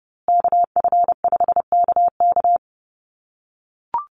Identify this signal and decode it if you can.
Morse code, heard as a keyed tone, about 30 wpm, 700 Hz. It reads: KF5KK
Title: USA Call Signs Repeated 3x 30wpm